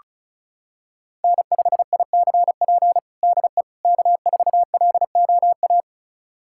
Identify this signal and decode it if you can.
Morse code, heard as a keyed tone, about 36 wpm, 700 Hz. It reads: N5ICP DE K4LOA
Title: USA Callsign DE Callsign - 5 Characters - 35wpm